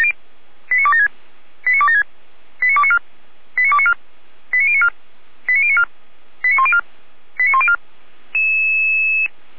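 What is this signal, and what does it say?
Five-tone selective call (ZVEI), heard as quick runs of five tones, tones of about 70 ms. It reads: 89276 89276 89294 89294 89094 89094 89104 89104
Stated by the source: ZVEI five-tone selective call 89262, received on SDR by Feuerwehr Orlamünde.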